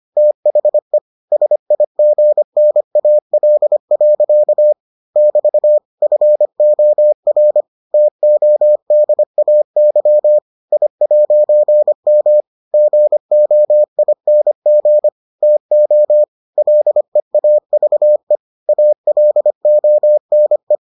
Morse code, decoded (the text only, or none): THE SIGNAL. <BT> FOR TODAY I'M GOING TO LEAVE ALONE